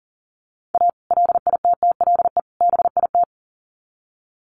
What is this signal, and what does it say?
Morse code, heard as a keyed tone, about 40 wpm, 700 Hz. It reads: A LITTLE BIT
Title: Sets of 3 Words 40wpm